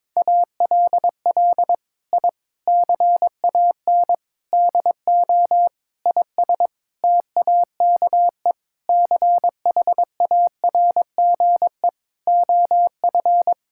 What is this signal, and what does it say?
Morse code, heard as a keyed tone, about 22 wpm, 700 Hz. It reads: ALL I CAN DO IS TAKE CHARGE OF
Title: Sentences from Top 1000 Words - Review Part 6 - 22wpm